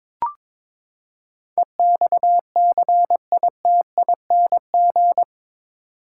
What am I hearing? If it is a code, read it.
Morse code, heard as a keyed tone, about 22 wpm, 700 Hz. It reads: EXCITING
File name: practice / 22wpm / Eight Letter Words - 22wpm